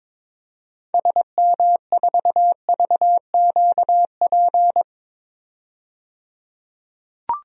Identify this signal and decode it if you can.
Morse code, heard as a keyed tone, about 22 wpm, 700 Hz. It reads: SM4VQP
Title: DXCC-100 Call Signs - 6 Characters - 22wpm